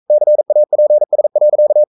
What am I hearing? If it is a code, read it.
Morse code, heard as a keyed tone, about 42 wpm, 600 Hz. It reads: CAPS.
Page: Romanen Never af Ken Follet på engelsk.